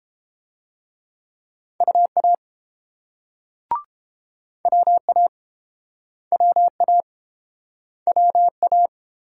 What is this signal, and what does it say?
Morse code, heard as a keyed tone, about 32 wpm, 700 Hz. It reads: UA WA WA WA